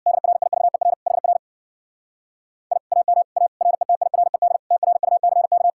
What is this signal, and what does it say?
Morse code, heard as a keyed tone, about 66 wpm, 700 Hz. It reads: DRIVER HR I AM A RETIRED TRUCK